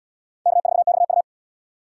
Morse code, heard as a keyed tone, about 66 wpm, 700 Hz. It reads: K4XU